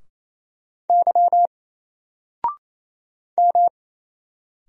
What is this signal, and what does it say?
Morse code, heard as a keyed tone, about 27 wpm, 700 Hz. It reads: Y M
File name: Single-Letter-Number-28wpm